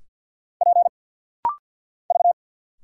Morse code, heard as a keyed tone, about 50 wpm, 700 Hz. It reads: P V
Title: Single Letters - Rapid-Fire 50wpm